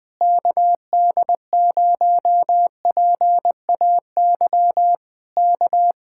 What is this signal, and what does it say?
Morse code, heard as a keyed tone, about 20 wpm, 700 Hz. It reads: KD0PAY K
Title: Speed-Racing Common QSO Phrases - 20wpm